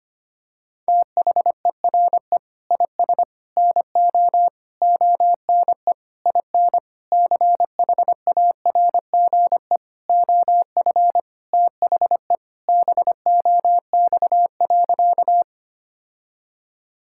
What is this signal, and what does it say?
Morse code, heard as a keyed tone, about 25 wpm, 700 Hz. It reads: THERE IS NO ONE IN CHARGE OF THE BOX.
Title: Sentences from Top 1000 Words - Review Part 6 - 25wpm